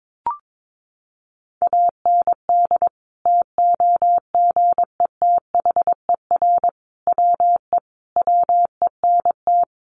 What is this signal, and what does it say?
Morse code, heard as a keyed tone, about 22 wpm, 700 Hz. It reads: AND TOGETHER WE WENT
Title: Sentences from Top 600 Words 22wpm